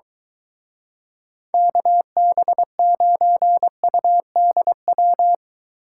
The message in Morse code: KB9UDW

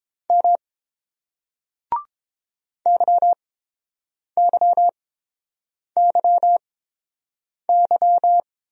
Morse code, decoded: M YYYY

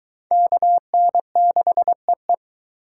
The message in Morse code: KN6EE

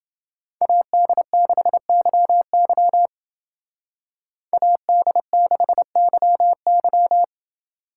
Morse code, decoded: AD6YY AD6YY

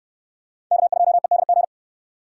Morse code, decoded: K2EDG

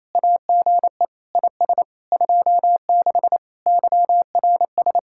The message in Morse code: AGE IS 26 YRS

